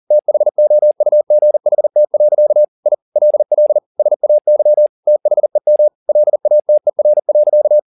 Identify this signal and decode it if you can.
Morse code, heard as a keyed tone, about 40 wpm, 600 Hz. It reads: THOUGHT. I LL SAY THEM LATER.
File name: the old man and the sea 2 40wpm